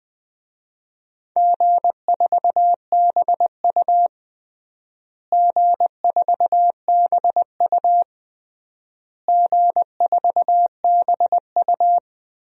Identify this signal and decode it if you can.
Morse code, heard as a keyed tone, about 20 wpm, 700 Hz. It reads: G4BU G4BU G4BU
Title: DXCC-100 Call Signs - 4 Characters - Repeated 3x - 20wpm